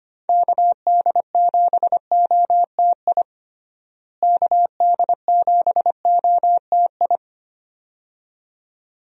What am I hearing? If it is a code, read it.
Morse code, heard as a keyed tone, about 25 wpm, 700 Hz. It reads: KD7OTI KD7OTI
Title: USA Call Signs - 6 Characters - Repeated 2x - 25wpm